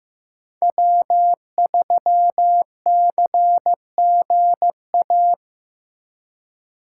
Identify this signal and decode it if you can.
Morse code, heard as a keyed tone, about 18 wpm, 700 Hz. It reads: W3CGA
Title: Speed Racing - USA 1x3 Call Signs 15wpm